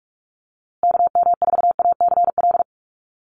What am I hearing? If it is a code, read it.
Morse code, heard as a keyed tone, about 44 wpm, 700 Hz. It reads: KM4ACL
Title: USA Call Signs Repeated 3x 45wpm